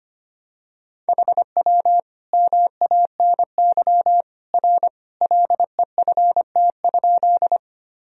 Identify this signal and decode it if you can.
Morse code, heard as a keyed tone, about 25 wpm, 700 Hz. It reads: HW MANY R LEFT?
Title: Sentences from Top 300 Words and Abbreviations 25wpm